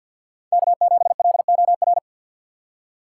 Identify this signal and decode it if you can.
Morse code, heard as a keyed tone, about 48 wpm, 700 Hz. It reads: K7LOR
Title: USA Call Signs - 2x Character Spacing - 50wpm